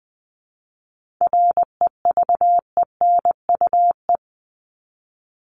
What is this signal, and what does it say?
Morse code, heard as a keyed tone, about 20 wpm, 700 Hz. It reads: REVENUE
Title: Top 1501-1750 Words 20wpm